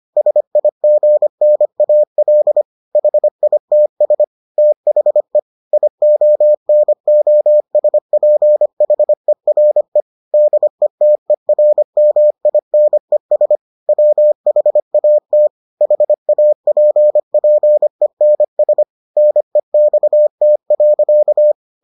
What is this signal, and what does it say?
Morse code, heard as a keyed tone, about 25 wpm, 600 Hz. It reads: SIGNAL HITS THE IONOSPHERE DETERMINES WHAT HAPPENS NEXT.